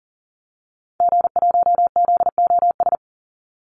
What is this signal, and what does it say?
Morse code, heard as a keyed tone, about 40 wpm, 700 Hz. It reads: G1ZOS
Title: Call Signs - 40wpm